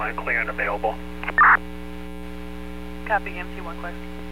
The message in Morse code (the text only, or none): none